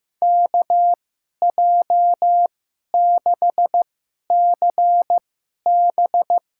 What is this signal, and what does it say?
Morse code, heard as a keyed tone, about 15 wpm, spaced as for 11 wpm, 700 Hz. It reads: KJ6CB
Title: USA Call Signs - 6 Characters - 2x Character Spacing - 15wpm